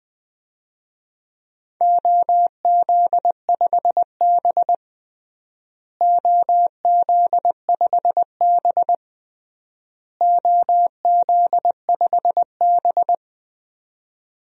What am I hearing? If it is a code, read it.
Morse code, heard as a keyed tone, about 20 wpm, 700 Hz. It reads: OZ5B OZ5B OZ5B
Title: DXCC-100 Call Signs - 4 Characters - Repeated 3x - 20wpm